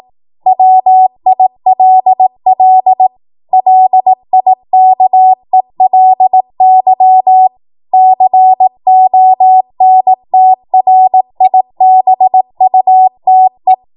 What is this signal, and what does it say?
Morse code, heard as a keyed tone, about 18 wpm, 750 Hz. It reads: WILL LIKELY CONTRIBUTE
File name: Morse / 18WPM